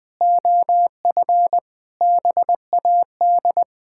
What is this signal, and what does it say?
Morse code, heard as a keyed tone, about 20 wpm, 700 Hz. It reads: OF BAD
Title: Sentences from Top 1000 Words - Review Part 6 - 20wpm